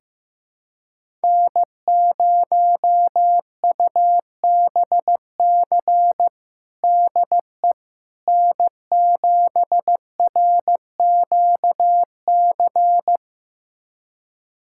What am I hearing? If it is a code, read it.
Morse code, heard as a keyed tone, about 15 wpm, 700 Hz. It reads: N0UBC DE N7RQC
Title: USA Callsign DE Callsign - 5 Characters - 15wpm